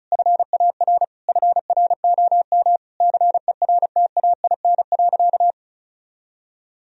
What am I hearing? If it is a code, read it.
Morse code, heard as a keyed tone, about 36 wpm, 700 Hz. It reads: FAR FROM CERTAIN.
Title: Sentences from Top 1000 Words - Review Part 2 - 35wpm